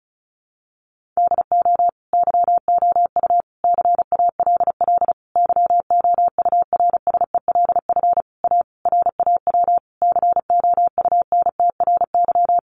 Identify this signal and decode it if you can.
Morse code, heard as a keyed tone, about 34 wpm, 700 Hz. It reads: DO YOU CALL YOURSELF A RAW COUNTRY
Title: Sentences from Top 1500 Words 35wpm